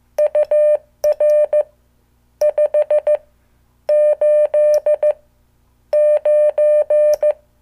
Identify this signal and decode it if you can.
Morse code, heard as a keyed tone, about 15 wpm, spaced as for 9 wpm, 600 Hz. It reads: UR 589
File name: morse / qso